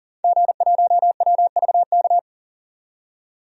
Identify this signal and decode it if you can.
Morse code, heard as a keyed tone, about 40 wpm, 700 Hz. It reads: G1WVK
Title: DXCC-100 Call Signs - 5 Characters - 40wpm